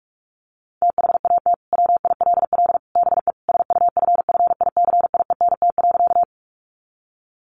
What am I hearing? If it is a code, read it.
Morse code, heard as a keyed tone, about 44 wpm, 700 Hz. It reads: THAT WILL BE SUFFICIENT.
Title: Sentences from Top 1400 Words 45wpm